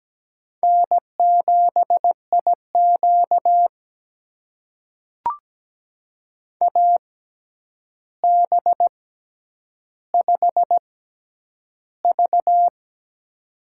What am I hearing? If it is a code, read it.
Morse code, heard as a keyed tone, about 17 wpm, spaced as for 9 wpm, 700 Hz. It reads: N7IQ AB5V
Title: USA Call Signs - 4 Characters - 6x Character Spacing - 17wpm